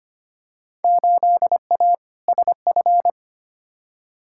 Morse code, heard as a keyed tone, about 25 wpm, 700 Hz. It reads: 8A SF
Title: Field Day Exchanges Encore - 25wpm